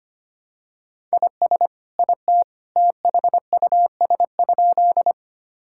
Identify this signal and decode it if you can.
Morse code, heard as a keyed tone, about 25 wpm, 700 Hz. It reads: IS IT THUS?
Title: Sentences from Top 1000 Words - Review Part 5 - 25wpm